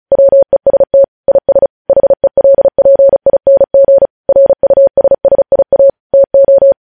Morse code, heard as a keyed tone, about 36 wpm, 550 Hz. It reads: WEST IS HELPING RUSSIA TO
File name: podcast / News headlines-1x-35x35